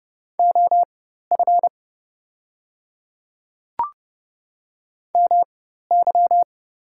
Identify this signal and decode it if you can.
Morse code, heard as a keyed tone, about 30 wpm, spaced as for 14 wpm, 700 Hz. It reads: OF MY
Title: 4x Character Spacing - Top 300 Words - 30wpm